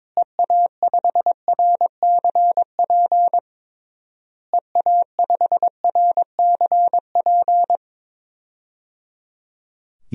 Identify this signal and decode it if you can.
Morse code, heard as a keyed tone, about 22 wpm, 700 Hz. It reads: EA5RCP EA5RCP
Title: DXCC-100 Call Signs - 6 Characters - Repeated 3x - 22wpm